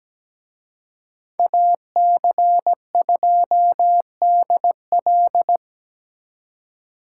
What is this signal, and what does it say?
Morse code, heard as a keyed tone, about 20 wpm, 700 Hz. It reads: AC2DL
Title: Speed Racing - USA Call Signs 17wpm